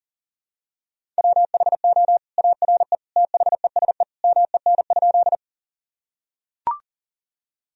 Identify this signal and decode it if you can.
Morse code, heard as a keyed tone, about 40 wpm, 700 Hz. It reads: WHO ARE THESE MEN?
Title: Sentences from Top 1000 Words - Review Part 1 - 40wpm